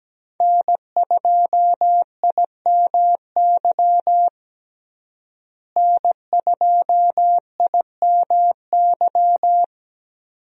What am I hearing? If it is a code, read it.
Morse code, heard as a keyed tone, about 17 wpm, 700 Hz. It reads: N2IMY N2IMY
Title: USA Call Signs - 5 Characters - Repeated 2x - 17wpm